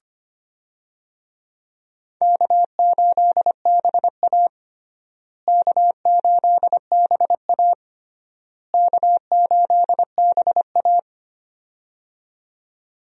Morse code, heard as a keyed tone, about 25 wpm, 700 Hz. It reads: K8BA K8BA K8BA
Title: USA Call Signs - 4 Characters - Repeated 3x - 25wpm